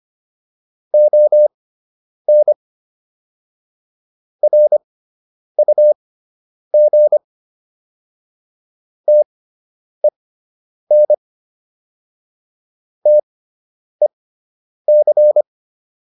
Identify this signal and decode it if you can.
Morse code, heard as a keyed tone, about 25 wpm, spaced as for 9 wpm, 600 Hz. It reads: ON RUG TEN TEC